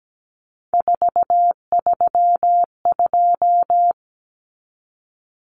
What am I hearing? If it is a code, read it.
Morse code, heard as a keyed tone, about 17 wpm, 690 Hz. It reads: 432
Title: Common Cut Numbers - 3-Digit 17wpm